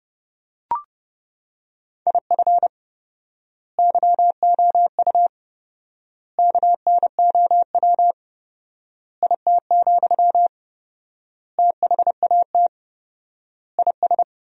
Morse code, heard as a keyed tone, about 30 wpm, 700 Hz. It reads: IF YOU KNOW IT, THAT IS
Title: 4x Word-Spacing - Sentences from Top 400 Words - 30wpm